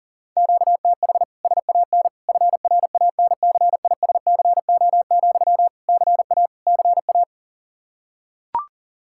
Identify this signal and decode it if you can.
Morse code, heard as a keyed tone, about 40 wpm, 700 Hz. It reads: QTH SAN FRANCISCO, CA CA
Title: K1USN SST QSO Practice 40wpm